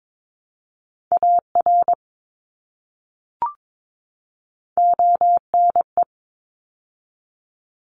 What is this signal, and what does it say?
Morse code, heard as a keyed tone, about 22 wpm, 700 Hz. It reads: AR ONE
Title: ARRL Sections 22wpm